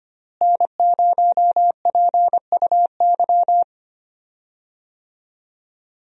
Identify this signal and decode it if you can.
Morse code, heard as a keyed tone, about 25 wpm, 700 Hz. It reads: N0PUY
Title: USA Call Signs - 5 Characters - Repeated 2x - 25wpm